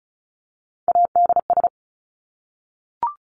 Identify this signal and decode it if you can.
Morse code, heard as a keyed tone, about 34 wpm, 700 Hz. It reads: ADS
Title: Abbreviations 35wpm